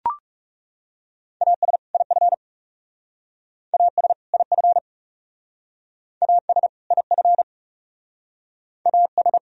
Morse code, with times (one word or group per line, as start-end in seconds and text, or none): 1.4-1.8 AS
1.9-2.3 IF
3.7-4.1 AS
4.3-4.8 IF
6.2-6.7 AS
6.9-7.4 IF
8.9-9.4 AS